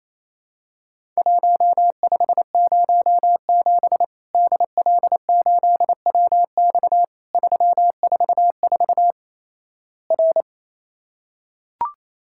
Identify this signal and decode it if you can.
Morse code, heard as a keyed tone, about 27 wpm, 700 Hz. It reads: 1507 DL8WX 344 R
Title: Worked All Europe DX Contest - QTC Exchange - 28wpm